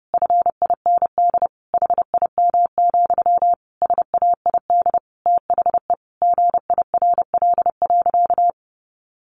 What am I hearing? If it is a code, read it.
Morse code, heard as a keyed tone, about 30 wpm, 700 Hz. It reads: FIND HIM, SAID THE GIRL.
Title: Sentences from Top 400 Words 30wpm